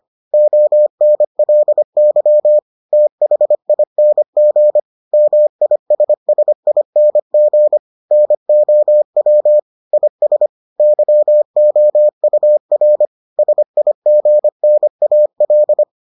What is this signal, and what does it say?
Morse code, heard as a keyed tone, about 25 wpm, 600 Hz. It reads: ONLY THING MISSING NOW IS YOUR SIGNAL